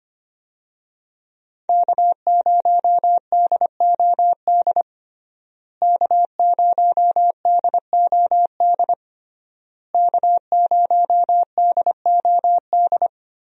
Morse code, heard as a keyed tone, about 25 wpm, 700 Hz. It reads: K0DOD K0DOD K0DOD
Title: USA Call Signs - 5 Characters - Repeated 3x - 25wpm